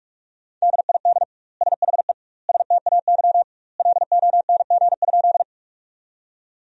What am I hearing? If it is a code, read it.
Morse code, heard as a keyed tone, about 46 wpm, 700 Hz. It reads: DID SHE STAY LONG?
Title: 2x Word-Spacing - Sentences from Top 500 Words - 45wpm